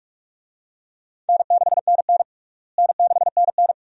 Morse code, heard as a keyed tone, about 46 wpm, 700 Hz. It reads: N6NN N6NN
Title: USA Call Signs - 4 Characters - Repeated 2x - 45wpm